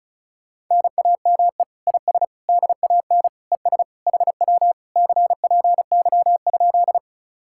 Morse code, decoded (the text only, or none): NAME IS DAN ES HW CPY?